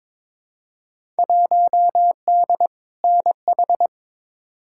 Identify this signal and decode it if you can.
Morse code, heard as a keyed tone, about 22 wpm, 700 Hz. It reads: 1D NH
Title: Field Day Exchanges Encore - 22wpm